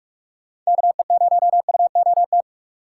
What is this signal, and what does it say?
Morse code, heard as a keyed tone, about 46 wpm, 700 Hz. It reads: KE0UOT
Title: USA Call Signs - 6 Characters - 2x Character Spacing - 45wpm